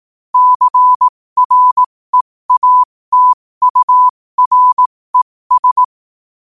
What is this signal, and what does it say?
Morse code, heard as a keyed tone, about 18 wpm, spaced as for 15 wpm, 1000 Hz. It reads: CREATURES
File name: qotdfiles / 15wpm